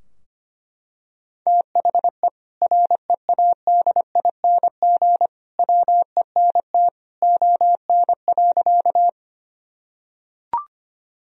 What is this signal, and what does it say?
Morse code, heard as a keyed tone, about 25 wpm, 700 Hz. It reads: THE READING WENT ON.
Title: Sentences from Top 1000 Words - Review Part 3 - 25wpm